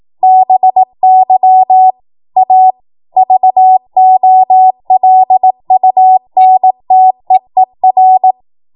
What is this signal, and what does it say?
Morse code, heard as a keyed tone, about 18 wpm, 750 Hz. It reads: BY A VOLUNTEER
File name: Morse / 18WPM